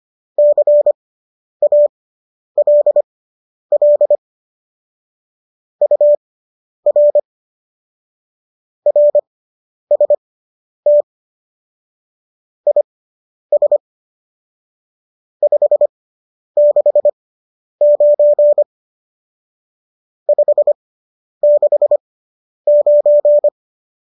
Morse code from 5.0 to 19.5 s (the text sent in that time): UR RST IS 569